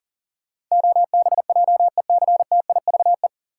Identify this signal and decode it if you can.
Morse code, heard as a keyed tone, about 37 wpm, 700 Hz. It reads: OBJECTIVE
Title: Sentences from Top 1200 Words - New Word Drill - 40wpm